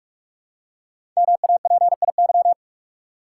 Morse code, heard as a keyed tone, about 44 wpm, 700 Hz. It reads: MAPIY
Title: Code Groups - 5 Letters - 45wpm